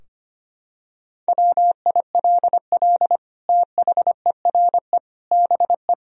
Morse code, heard as a keyed tone, about 25 wpm, 700 Hz. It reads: WILL THERE BE
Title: Sentences from Top 1000 Words - Review Part 2 - 25wpm